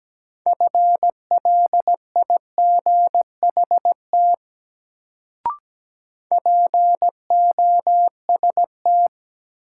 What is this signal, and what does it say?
Morse code, heard as a keyed tone, about 17 wpm, 700 Hz. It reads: FLIGHT POST
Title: Sentences from Top 1000 Words - New Word Drill - 17wpm